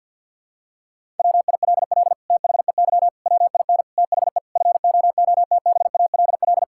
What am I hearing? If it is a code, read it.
Morse code, heard as a keyed tone, about 50 wpm, 700 Hz. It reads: WILL THEY WIN THE FOOTBALL
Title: Sentences from Top 1600 Words 50wpm